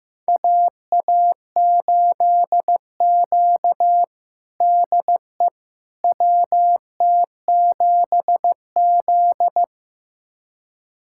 Morse code, heard as a keyed tone, about 15 wpm, 700 Hz. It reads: AA8Q DE WT7Z
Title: USA Callsign DE Callsign - 4 Characters - 15wpm